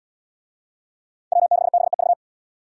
Morse code, heard as a keyed tone, about 77 wpm, 700 Hz. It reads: W6BEV